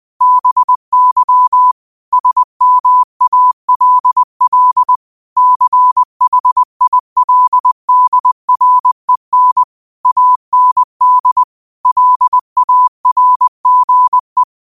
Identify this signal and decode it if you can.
Morse code, heard as a keyed tone, about 20 wpm, 1000 Hz. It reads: BY SMALL CHILDREN AND LARGE
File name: qotdfiles / 20wpm